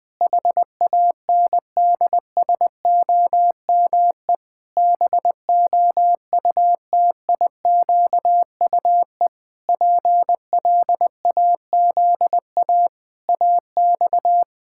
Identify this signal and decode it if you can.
Morse code, heard as a keyed tone, about 20 wpm, 700 Hz. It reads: HANDSOME BOUTIQUE PLAZA AX
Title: Pangram Challenge v2 - 20wpm